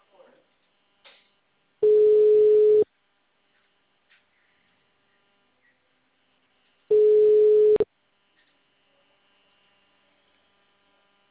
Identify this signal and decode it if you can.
Ringback tone (Europe, 425 Hz), with speech.
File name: out- -2001- - - .56426